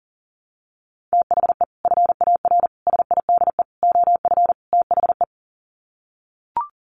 Morse code, heard as a keyed tone, about 40 wpm, 700 Hz. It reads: THE FAR SIDE OF THE
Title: Sets of 5 Words 40wpm